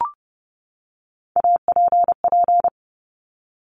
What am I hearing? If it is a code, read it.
Morse code, heard as a keyed tone, about 30 wpm, 700 Hz. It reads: APP
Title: Top Three Letter Combinations in Words 30wpm